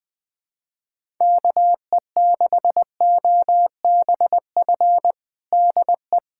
Morse code, heard as a keyed tone, about 20 wpm, 700 Hz. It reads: KE6OBF DE
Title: USA Callsign DE Callsign 20wpm